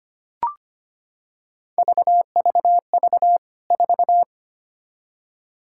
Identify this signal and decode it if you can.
Morse code, heard as a keyed tone, about 25 wpm, 700 Hz. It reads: VVV 4
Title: K1USN SST QSO Practice 25wpm